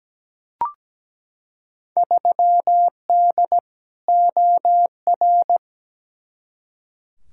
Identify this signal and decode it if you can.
Morse code, heard as a keyed tone, about 17 wpm, 700 Hz. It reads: 3D OR
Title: Field Day Exchanges - 17wpm